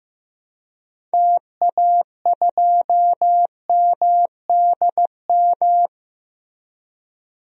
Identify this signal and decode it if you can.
Morse code, heard as a keyed tone, about 15 wpm, 700 Hz. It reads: TA2MDM
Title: DXCC-100 Call Signs - 6 Characters - Repeated 2x - 15wpm